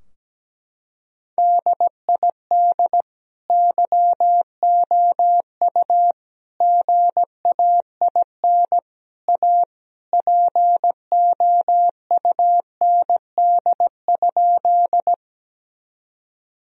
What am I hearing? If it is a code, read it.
Morse code, heard as a keyed tone, about 17 wpm, 700 Hz. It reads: DID YOU GAIN A POUND?
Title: Sentences from Top 1000 Words - Review Part 2 - 17wpm